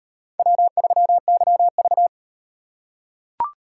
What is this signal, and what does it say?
Morse code, heard as a keyed tone, about 37 wpm, 700 Hz. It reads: W3YV